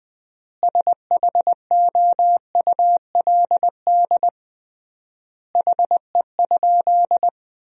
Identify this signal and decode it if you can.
Morse code, heard as a keyed tone, about 21 wpm, 700 Hz. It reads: SHOULD HE?
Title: 3x Word-Spacing - Sentences from Top 300 Words - 20wpm